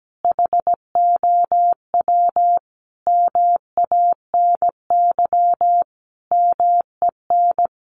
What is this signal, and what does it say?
Morse code, heard as a keyed tone, about 17 wpm, 700 Hz. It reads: HOW MANY MEN
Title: Sentences from Top 1400 Words 17wpm